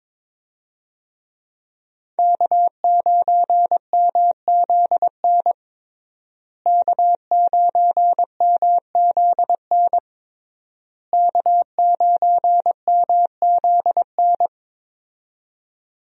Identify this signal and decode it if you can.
Morse code, heard as a keyed tone, about 22 wpm, 700 Hz. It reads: K9MZN K9MZN K9MZN
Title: USA Call Signs - 5 Characters - Repeated 3x - 22wpm